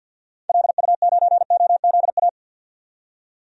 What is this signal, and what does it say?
Morse code, heard as a keyed tone, about 47 wpm, 700 Hz. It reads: LU9OZA
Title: DXCC-100 Call Signs - 6 Characters - 50wpm